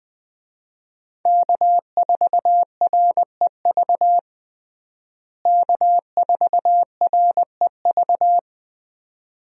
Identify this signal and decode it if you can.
Morse code, heard as a keyed tone, about 20 wpm, 700 Hz. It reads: K4REV K4REV
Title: USA Call Signs - 5 Characters - Repeated 2x - 20wpm